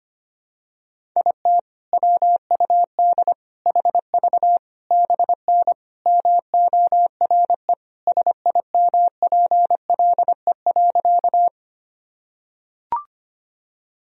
Morse code, heard as a keyed tone, about 25 wpm, 700 Hz. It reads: IT WUD HV BN MORE SIMPLE.
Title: Sentences from Top 400 Words and Abbreviations 25wpm